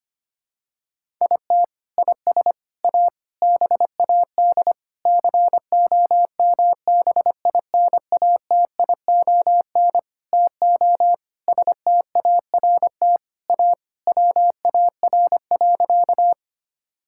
Morse code, heard as a keyed tone, about 25 wpm, 700 Hz. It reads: IT IS A BAD COMBINATION TO START A WAR.
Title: Sentences from Top 1000 Words - Review Part 6 - 25wpm